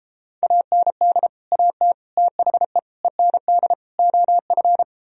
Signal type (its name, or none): Morse code